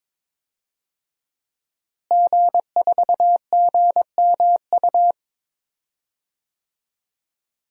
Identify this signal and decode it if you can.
Morse code, heard as a keyed tone, about 22 wpm, 700 Hz. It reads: G4GMU
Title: DXCC-100 Call Signs - 5 Characters - Repeated 3x - 22wpm